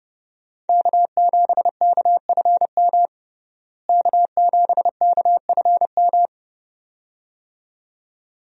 Morse code, heard as a keyed tone, about 31 wpm, 700 Hz. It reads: K7KFM K7KFM
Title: USA Call Signs - 5 Characters - Repeated 2x - 30wpm